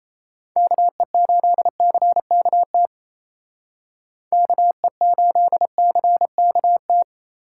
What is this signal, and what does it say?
Morse code, heard as a keyed tone, about 36 wpm, 700 Hz. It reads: KE8CKT KE8CKT